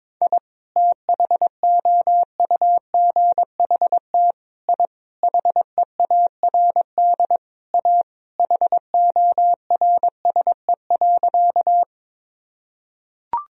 Morse code, heard as a keyed tone, about 22 wpm, 700 Hz. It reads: I THOUGHT I HEARD A HORSE.
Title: Sentences from Top 1000 Words - Review Part 1 - 22wpm